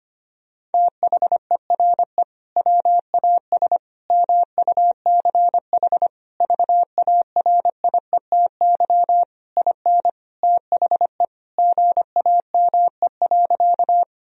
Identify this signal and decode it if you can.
Morse code, heard as a keyed tone, about 25 wpm, 700 Hz. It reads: THERE WAS MUCH VARIETY IN THE GAME.
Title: Sentences from Top 1000 Words - Review Part 4 - 25wpm